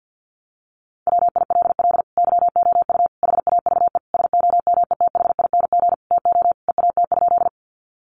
Morse code, heard as a keyed tone, about 50 wpm, 700 Hz. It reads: WILL YOU HAVE SOMETHING TO EAT?
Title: Sentences from Top 600 Words 50wpm